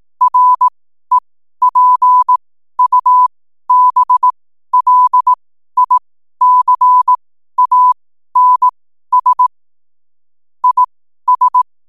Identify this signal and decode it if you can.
Morse code, heard as a keyed tone, about 18 wpm, spaced as for 12 wpm, 1000 Hz. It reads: REPUBLICANS IS
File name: n7net / 12wpm